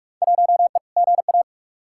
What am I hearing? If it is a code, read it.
Morse code, heard as a keyed tone, about 41 wpm, 700 Hz. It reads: 1E GA
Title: Field Day Exchanges - 45wpm